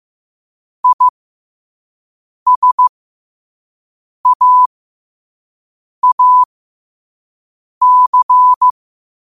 Morse code, heard as a keyed tone, about 15 wpm, spaced as for 5 wpm, 1000 Hz. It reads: ISAAC
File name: qotdfiles / 5wpm